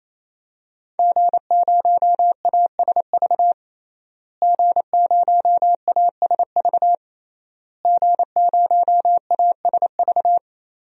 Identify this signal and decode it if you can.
Morse code, heard as a keyed tone, about 28 wpm, 700 Hz. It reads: G0ASV G0ASV G0ASV
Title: DXCC-100 Call Signs - 5 Characters - Repeated 3x - 28wpm